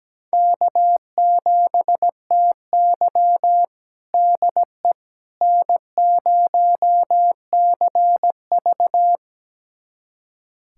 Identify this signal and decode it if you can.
Morse code, heard as a keyed tone, about 17 wpm, 700 Hz. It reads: K7TY DE N0CV
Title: USA Callsign DE Callsign - 4 Characters - 17wpm